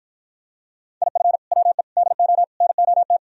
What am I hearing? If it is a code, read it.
Morse code, heard as a keyed tone, about 52 wpm, 700 Hz. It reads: IF WE DO NOT